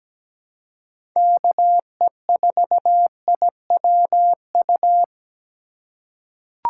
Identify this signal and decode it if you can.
Morse code, heard as a keyed tone, about 17 wpm, 700 Hz. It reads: KE4IWU
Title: USA Call Signs - 6 Characters - 3x Character Spacing - 17wpm